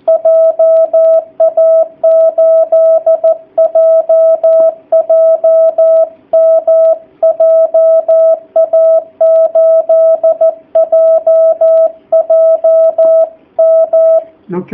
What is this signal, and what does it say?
Morse code, heard as a keyed tone, about 14 wpm, 660 Hz. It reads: JA8JJM JA8JJM
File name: jh8cbh / 2019cw / sp13- w